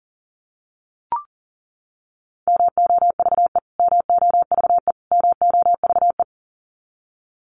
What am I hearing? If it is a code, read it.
Morse code, heard as a keyed tone, about 40 wpm, 700 Hz. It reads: MOVE MOVE MOVE
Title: Top 500 Words Repeated 40wpm